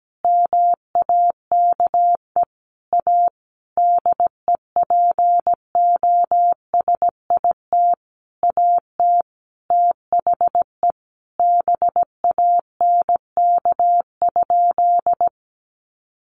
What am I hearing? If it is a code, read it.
Morse code, heard as a keyed tone, about 17 wpm, 700 Hz. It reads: MAKE A DEPOSIT AT THE BANK?
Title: Sentences from Top 1300 Words 17wpm